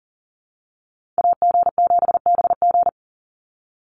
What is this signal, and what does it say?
Morse code, heard as a keyed tone, about 40 wpm, 700 Hz. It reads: AG7BG
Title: USA Call Signs 40wpm